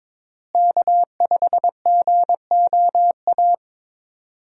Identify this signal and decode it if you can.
Morse code, heard as a keyed tone, about 22 wpm, 700 Hz. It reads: K5GOA
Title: USA Call Signs - 5 Characters - Repeated 3x - 22wpm